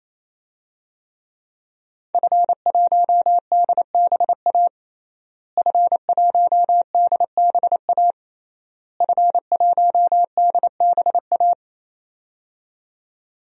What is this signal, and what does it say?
Morse code, heard as a keyed tone, about 28 wpm, 700 Hz. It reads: F1DBA F1DBA F1DBA
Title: DXCC-100 Call Signs - 5 Characters - Repeated 3x - 28wpm